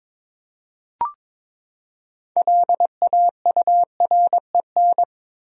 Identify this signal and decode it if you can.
Morse code, heard as a keyed tone, about 22 wpm, 700 Hz. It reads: LAUREN
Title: USA Common Names - 22wpm